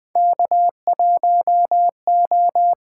Morse code, heard as a keyed tone, about 20 wpm, 700 Hz. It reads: K 1 O
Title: USA Call Signs - 8x Character Spacing - 20wpm